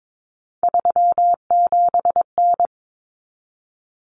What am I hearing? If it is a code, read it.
Morse code, heard as a keyed tone, about 22 wpm, 700 Hz. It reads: 37N
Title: Common Cut Numbers - 3-Digit 22wpm